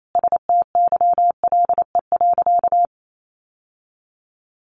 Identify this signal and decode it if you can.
Morse code, heard as a keyed tone, about 28 wpm, 700 Hz. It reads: STYLE.
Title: Sentences from Top 800 Words 28wpm